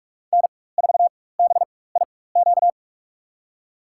Morse code, heard as a keyed tone, about 46 wpm, spaced as for 21 wpm, 700 Hz. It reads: N4BIQ
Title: USA Call Signs - 5 Characters - 4x Character Spacing - 45wpm